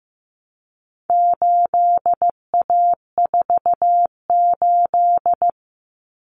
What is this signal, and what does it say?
Morse code, heard as a keyed tone, about 15 wpm, 700 Hz. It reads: 8A48
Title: Common Cut Numbers - 4-Digit 15wpm